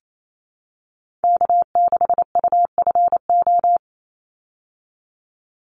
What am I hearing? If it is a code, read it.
Morse code, heard as a keyed tone, about 28 wpm, 700 Hz. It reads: K6UFO
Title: Call Signs Repeated 3x - 28wpm